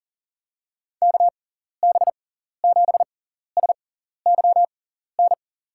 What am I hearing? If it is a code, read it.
Morse code, heard as a keyed tone, about 39 wpm, spaced as for 14 wpm, 700 Hz. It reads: KB7SYN
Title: USA Call Signs - 6 Characters - 6x Character Spacing - 40wpm